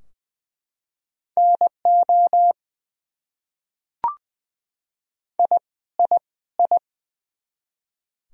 Morse code, heard as a keyed tone, about 20 wpm, 700 Hz. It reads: NO I I I
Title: Top 100 Words Repeated 20wpm